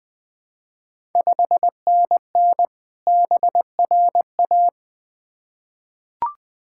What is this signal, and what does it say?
Morse code, heard as a keyed tone, about 20 wpm, 700 Hz. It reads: 5NN BRA
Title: Ontario QSO Party Exchange 20wpm